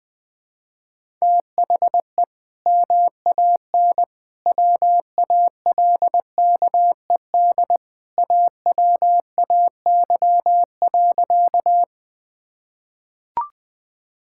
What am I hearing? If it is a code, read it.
Morse code, heard as a keyed tone, about 20 wpm, 700 Hz. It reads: THE MAN WALKED AWAY.
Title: Sentences with 2-Word Phrases - 20wpm